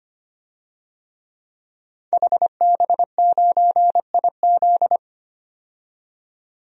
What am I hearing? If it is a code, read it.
Morse code, heard as a keyed tone, about 25 wpm, 700 Hz. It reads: HB9IZ
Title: DXCC-100 Call Signs - 5 Characters - 25wpm